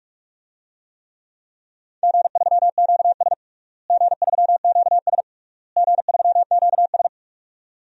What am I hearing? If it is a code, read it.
Morse code, heard as a keyed tone, about 46 wpm, 700 Hz. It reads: G3QS G3QS G3QS
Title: DXCC-100 Call Signs - 4 Characters - Repeated 3x - 45wpm